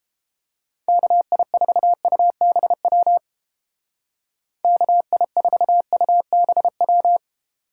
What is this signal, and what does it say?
Morse code, heard as a keyed tone, about 30 wpm, 700 Hz. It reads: KI4UBW KI4UBW